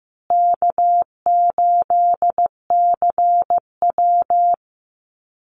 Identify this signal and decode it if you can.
Morse code, heard as a keyed tone, about 15 wpm, 700 Hz. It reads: K8CW
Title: USA Call Signs Repeated 3x 15wpm